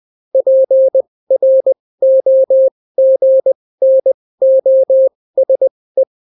Morse code, heard as a keyed tone, about 20 wpm, spaced as for 16 wpm, 530 Hz. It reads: PROGNOSE